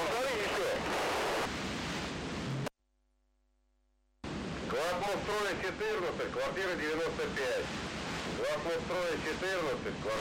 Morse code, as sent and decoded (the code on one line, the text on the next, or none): none